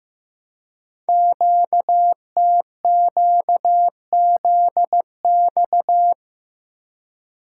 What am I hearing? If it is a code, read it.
Morse code, heard as a keyed tone, about 15 wpm, 700 Hz. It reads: QTQZX
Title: Code Groups - 5 Letters - 15wpm